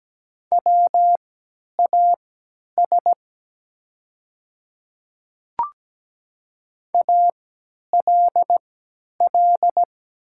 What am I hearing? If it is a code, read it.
Morse code, heard as a keyed tone, about 17 wpm, spaced as for 10 wpm, 700 Hz. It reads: WAS ALL
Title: 3x Character Spacing - Top 300 Words - 17wpm